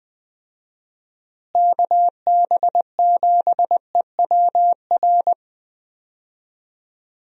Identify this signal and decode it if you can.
Morse code, heard as a keyed tone, about 20 wpm, 700 Hz. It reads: KB7EWR
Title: USA Call Signs - 6 Characters - Repeated 2x - 20wpm